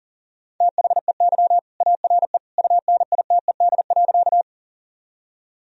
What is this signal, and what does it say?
Morse code, heard as a keyed tone, about 40 wpm, 700 Hz. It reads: THEY ARE UNITED.
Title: Sentences from Top 1000 Words - Review Part 4 - 40wpm